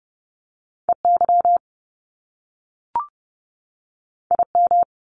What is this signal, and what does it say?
Morse code, heard as a keyed tone, about 30 wpm, 700 Hz. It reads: EY IM
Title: Top Two Letter Combinations in Words 30wpm